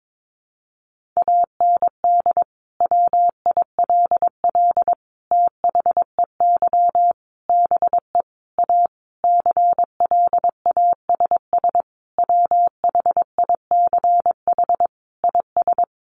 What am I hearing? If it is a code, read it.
Morse code, heard as a keyed tone, about 22 wpm, 700 Hz. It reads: AND WILL THEY BE A CLASS WHICH IS
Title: Sentences from Top 1400 Words 22wpm